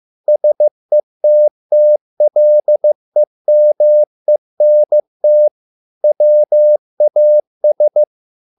Morse code, heard as a keyed tone, about 15 wpm, 600 Hz. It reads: SETTLEMENT WAS